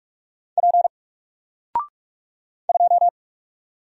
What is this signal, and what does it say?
Morse code, heard as a keyed tone, about 44 wpm, 700 Hz. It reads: P 2
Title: Single Character (Letter:Number:Punctuation:Prosign) 45wpm